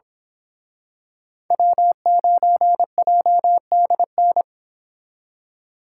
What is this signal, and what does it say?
Morse code, heard as a keyed tone, about 26 wpm, 700 Hz. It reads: W9JDN
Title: Speed Racing - USA 1x3 Call Signs 22wpm